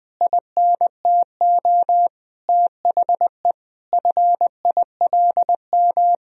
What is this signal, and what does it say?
Morse code, heard as a keyed tone, about 21 wpm, 700 Hz. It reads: INTO THE FILM
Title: Sentences from Top 1000 Words - Review Part 3 - 20wpm